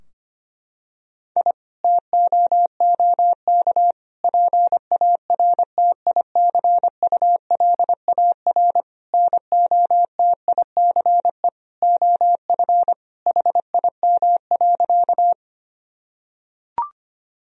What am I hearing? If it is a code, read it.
Morse code, heard as a keyed tone, about 25 wpm, 700 Hz. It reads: I TOOK PARTICULAR NOTICE OF HIM.
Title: Sentences from Top 1000 Words - Review Part 4 - 25wpm